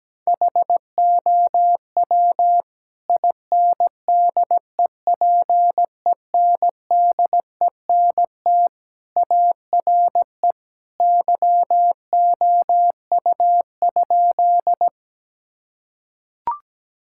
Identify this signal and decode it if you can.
Morse code, heard as a keyed tone, about 17 wpm, 700 Hz. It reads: HOW INDEPENDENT ARE YOU?
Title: Sentences from Top 1000 Words - Review Part 6 - 17wpm